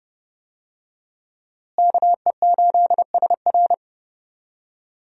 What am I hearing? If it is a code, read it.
Morse code, heard as a keyed tone, about 30 wpm, 700 Hz. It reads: KE8SR
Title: USA Call Signs - 5 Characters - Repeated 3x - 30wpm